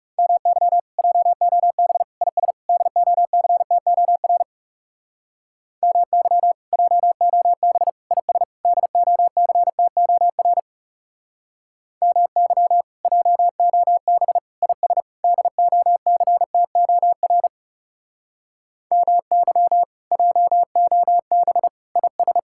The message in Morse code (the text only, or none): MY JOB IS DOCTOR MY JOB IS DOCTOR MY JOB IS DOCTOR MY JOB IS